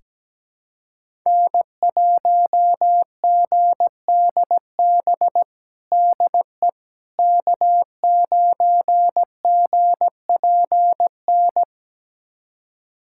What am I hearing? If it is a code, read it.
Morse code, heard as a keyed tone, about 17 wpm, 700 Hz. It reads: N1GDB DE K9GPN
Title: USA Callsign DE Callsign - 5 Characters - 17wpm